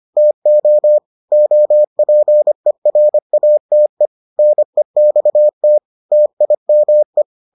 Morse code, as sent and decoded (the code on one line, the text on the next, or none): - --- / --- .--. .-. .- - . / -. -..- - / - .. -- .
TO OPERATE NEXT TIME